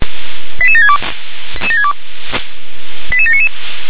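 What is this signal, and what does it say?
Five-tone selective call (ZVEI), heard as quick runs of five tones, tones of about 70 ms. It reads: 88762 88799
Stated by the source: ZVEI five-tone call received on SDR.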